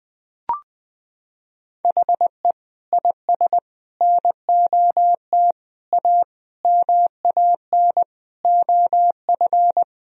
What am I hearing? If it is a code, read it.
Morse code, heard as a keyed tone, about 21 wpm, 700 Hz. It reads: HE IS NOT A MAN OF
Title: Sentences from Top 1000 Words - Review Part 4 - 20wpm